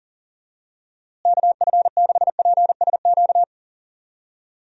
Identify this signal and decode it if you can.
Morse code, heard as a keyed tone, about 40 wpm, 700 Hz. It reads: KF6PSQ
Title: USA Call Signs - 6 Characters - Repeated 2x - 40wpm